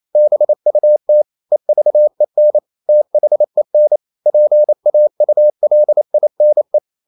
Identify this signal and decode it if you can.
Morse code, heard as a keyed tone, about 28 wpm, 600 Hz. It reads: BUT EVEN THEN PAULINE
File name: Never 0027wpm